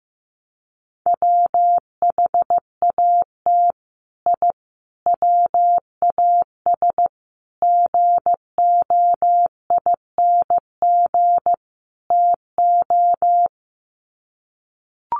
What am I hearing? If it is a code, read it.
Morse code, heard as a keyed tone, about 15 wpm, 700 Hz. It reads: WHAT I WAS GOING TO
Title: Sets of 5 Words Encore2 15wpm